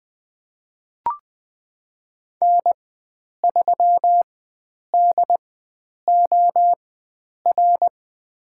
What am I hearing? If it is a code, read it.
Morse code, heard as a keyed tone, about 20 wpm, spaced as for 9 wpm, 700 Hz. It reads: N3DOR
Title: USA Call Signs - 5 Characters - 4x Character Spacing - 20wpm